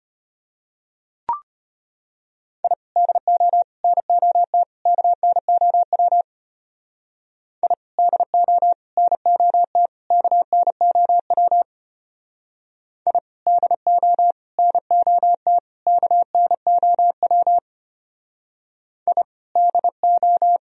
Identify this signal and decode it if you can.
Morse code, heard as a keyed tone, about 29 wpm, 700 Hz. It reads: I DO NOT KNOW I DO NOT KNOW I DO NOT KNOW I DO
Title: Speed-Racing - Sets of 4 Words 25wpm